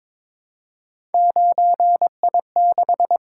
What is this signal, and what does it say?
Morse code, heard as a keyed tone, about 22 wpm, 700 Hz. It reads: 9I6
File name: practice / 22wpm / Three Letter-Number Combinations - 22wpm